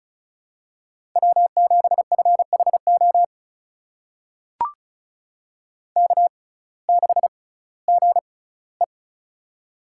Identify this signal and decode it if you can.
Morse code, heard as a keyed tone, about 36 wpm, spaced as for 19 wpm, 700 Hz. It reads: W7FHO K6GE
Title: USA Call Signs - 6x Character Spacing - 35wpm